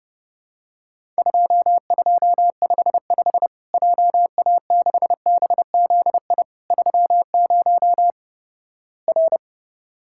Morse code, heard as a keyed tone, about 30 wpm, 700 Hz. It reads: 2255 JA6BZI 30 R
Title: Worked All Europe DX Contest - QTC Exchange - 30wpm